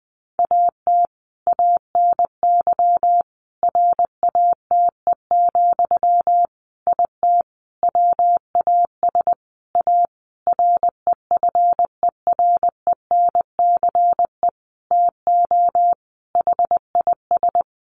Morse code, heard as a keyed tone, about 20 wpm, 700 Hz. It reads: AT ANY RATE, IT WAS A REFERENCE TO HIS
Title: Sentences from Top 1200 Words 20wpm